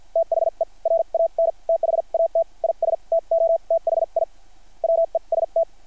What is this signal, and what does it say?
Morse code, heard as a keyed tone, about 50 wpm, spaced as for 41 wpm, 660 Hz. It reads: THE RAN THAT IS TOTHI WEST
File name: archived / CRSnet